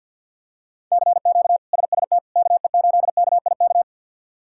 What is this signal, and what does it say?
Morse code, heard as a keyed tone, about 47 wpm, 700 Hz. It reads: CQ SST KE8KIK
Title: K1USN SST QSO Practice - 50wpm